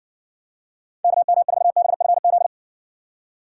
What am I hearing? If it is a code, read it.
Morse code, heard as a keyed tone, about 60 wpm, 700 Hz. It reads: KM4BFZ